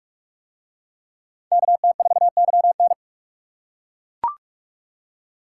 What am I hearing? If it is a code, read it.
Morse code, heard as a keyed tone, about 41 wpm, 700 Hz. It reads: KT4YN